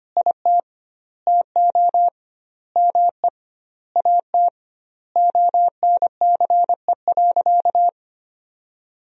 Morse code, heard as a keyed tone, about 25 wpm, 700 Hz. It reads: IT TO ME AT ONCE.
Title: 2x Word-Spacing - Sentences from Top 500 Words - 25wpm